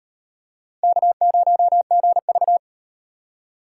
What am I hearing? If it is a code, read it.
Morse code, heard as a keyed tone, about 36 wpm, 700 Hz. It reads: K0GV